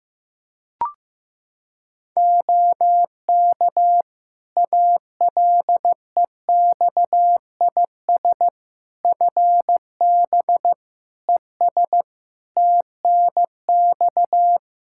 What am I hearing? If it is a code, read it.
Morse code, heard as a keyed tone, about 15 wpm, 700 Hz. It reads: OK ALEXIS FB ES TNX
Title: Practice QSO - Exchange 2 - Info - 15wpm